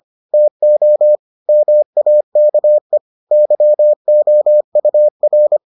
Morse code, heard as a keyed tone, about 25 wpm, 600 Hz. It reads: TO MAKE YOUR